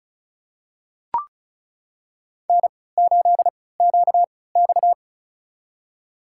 Morse code, heard as a keyed tone, about 36 wpm, spaced as for 20 wpm, 700 Hz. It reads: N8QX
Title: USA Call Signs - 4 Characters - 3x Character Spacing - 35wpm